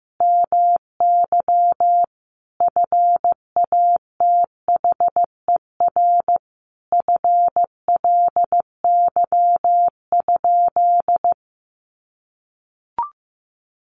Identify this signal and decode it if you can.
Morse code, heard as a keyed tone, about 15 wpm, 700 Hz. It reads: MY FATHER FLY?
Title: Sentences from Top 500 Words 15wpm